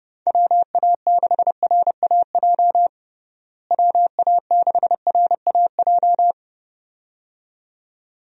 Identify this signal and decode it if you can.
Morse code, heard as a keyed tone, about 30 wpm, 700 Hz. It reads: WA6RAJ WA6RAJ
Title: USA Call Signs - 6 Characters - Repeated 2x - 30wpm